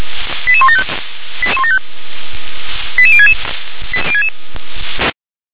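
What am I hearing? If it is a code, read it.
Five-tone selective call (ZVEI), heard as quick runs of five tones, tones of about 70 ms. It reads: 88175 88066
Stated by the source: ZVEI five-tone selective call received by SDR.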